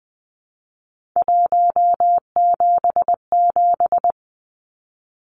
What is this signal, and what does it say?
Morse code, heard as a keyed tone, about 20 wpm, 700 Hz. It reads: 177
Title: Numbers - 3-Digit 20wpm